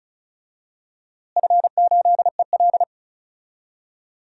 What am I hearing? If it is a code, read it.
Morse code, heard as a keyed tone, about 34 wpm, 700 Hz. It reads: F8EL
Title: DXCC-100 Call Signs - 4 Characters - Repeated 3x - 35wpm